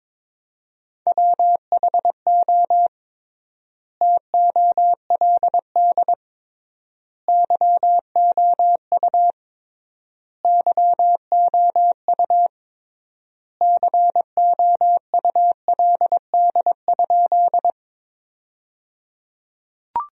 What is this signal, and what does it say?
Morse code, heard as a keyed tone, about 22 wpm, 700 Hz. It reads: WHO TOLD YOU YOU COULD?
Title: 3x Word-Spacing - Sentences from Top 400 Words - 22wpm